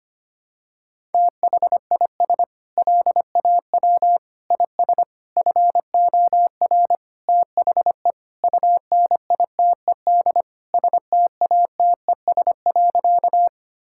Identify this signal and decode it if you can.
Morse code, heard as a keyed tone, about 25 wpm, 700 Hz. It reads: THIS LAW IS FOR THE UNITED STATES.
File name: practice / 25wpm / Sentences with 4-Word Phrases - 25wpm